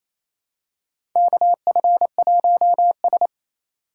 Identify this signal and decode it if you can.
Morse code, heard as a keyed tone, about 28 wpm, 700 Hz. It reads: KF1S
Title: USA Call Signs - 4 Characters - 2x Character Spacing - 28wpm